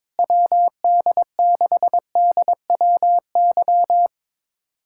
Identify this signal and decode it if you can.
Morse code, heard as a keyed tone, about 22 wpm, 700 Hz. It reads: WD6DWY
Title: USA Callsign DE Callsign - 6 Characters - 22wpm